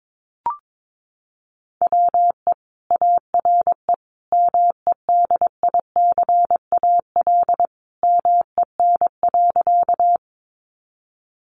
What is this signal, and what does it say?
Morse code, heard as a keyed tone, about 22 wpm, 700 Hz. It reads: WE ARE MEDICAL MEN.
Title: Sentences from Top 800 Words 22wpm